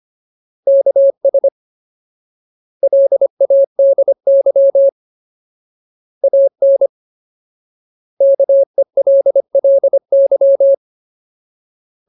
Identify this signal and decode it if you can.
Morse code, heard as a keyed tone, about 25 wpm, 550 Hz. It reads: KS LADY AN KELLY
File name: Ultimate Word List / 25WPM Ultimate